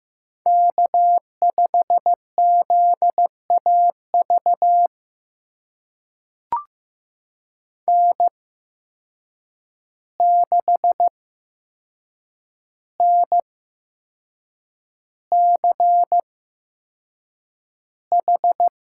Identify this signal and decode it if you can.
Morse code, heard as a keyed tone, about 15 wpm, spaced as for 6 wpm, 700 Hz. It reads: K5ZAV N6NCH
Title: USA Call Signs - 5 Characters - 8x Character Spacing - 15wpm